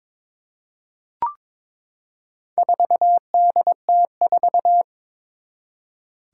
Morse code, heard as a keyed tone, about 21 wpm, 700 Hz. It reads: 4DT4
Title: Four Challenging Letter-Number Combinations 22wpm